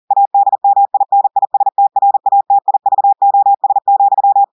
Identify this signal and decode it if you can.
Morse code, heard as a keyed tone, about 40 wpm, 800 Hz. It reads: ADMINISTRATIVOS,
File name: audios / portugais / portugais texte en clair 2 vitesse 40 mots minutes